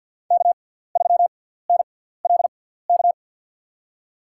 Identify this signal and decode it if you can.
Morse code, heard as a keyed tone, about 50 wpm, spaced as for 17 wpm, 700 Hz. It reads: K3NLK
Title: USA Call Signs - 6x Character Spacing - 50wpm